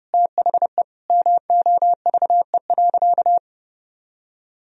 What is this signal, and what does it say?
Morse code, heard as a keyed tone, about 30 wpm, 700 Hz. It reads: THE MOVE.
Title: Sentences from Top 1000 Words - Review Part 1 - 30wpm